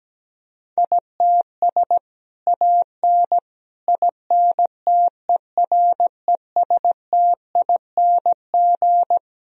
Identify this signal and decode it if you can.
Morse code, heard as a keyed tone, about 17 wpm, 700 Hz. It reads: ITS AN INTERESTING
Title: Sentences from Top 1000 Words - Review Part 4 - 17wpm